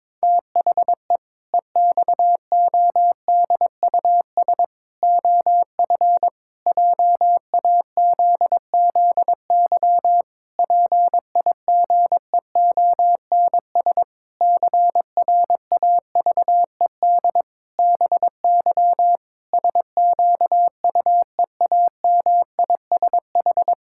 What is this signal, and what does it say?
Morse code, heard as a keyed tone, about 22 wpm, 700 Hz. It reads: THE EXODUS OF JAZZY PIGEONS CRAVED BY SQUEAMISH
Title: Pangram Challenge v2 - 22wpm